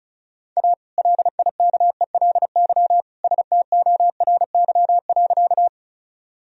Morse code, heard as a keyed tone, about 34 wpm, 700 Hz. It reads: A LIKELY STORY.
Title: Sentences from Top 1000 Words - Review Part 4 - 35wpm